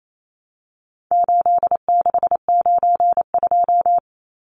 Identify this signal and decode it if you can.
Morse code, heard as a keyed tone, about 28 wpm, 700 Hz. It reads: 8692
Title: Numbers - 4-Digit 28wpm